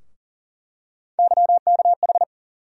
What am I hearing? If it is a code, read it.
Morse code, heard as a keyed tone, about 40 wpm, 700 Hz. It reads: YKH
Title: USA Call Sign Postfixes - 3 Letters 40wpm